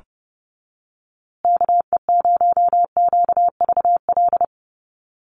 Morse code, heard as a keyed tone, about 29 wpm, 700 Hz. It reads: KE0QVL
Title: USA Call Signs Repeated 2x 30wpm